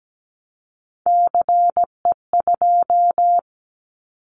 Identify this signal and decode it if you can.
Morse code, heard as a keyed tone, about 17 wpm, 700 Hz. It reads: CE2
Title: Callsign Prefixes for Top 100 DXCC - 17wpm